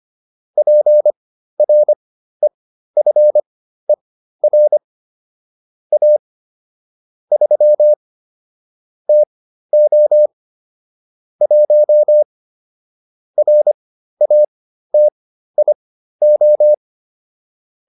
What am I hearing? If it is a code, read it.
Morse code, heard as a keyed tone, about 25 wpm, spaced as for 13 wpm, 600 Hz. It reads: PREFER A 3 TO 1 RATIO